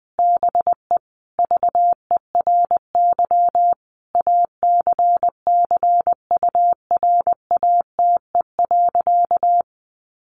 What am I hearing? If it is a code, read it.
Morse code, heard as a keyed tone, about 20 wpm, 700 Hz. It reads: BE VERY ACCURATE.
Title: Sentences from Top 1200 Words 20wpm